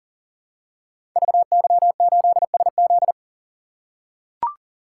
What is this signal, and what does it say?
Morse code, heard as a keyed tone, about 38 wpm, 700 Hz. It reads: VY8SZ
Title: Code Groups - 5 Letter:Numbers - 40wpm